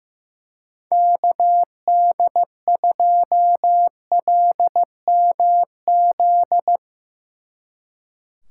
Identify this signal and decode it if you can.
Morse code, heard as a keyed tone, about 16 wpm, 700 Hz. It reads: KD2LMZ
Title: Speed Racing - USA Call Signs 15wpm